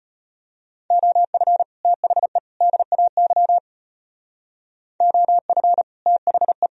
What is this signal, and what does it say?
Morse code, heard as a keyed tone, about 38 wpm, 700 Hz. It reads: OF THE DAY OF THE